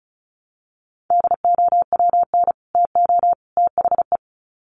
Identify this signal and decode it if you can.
Morse code, heard as a keyed tone, about 34 wpm, 700 Hz. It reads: DOWN TO THE
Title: Sets of 3 Words 35wpm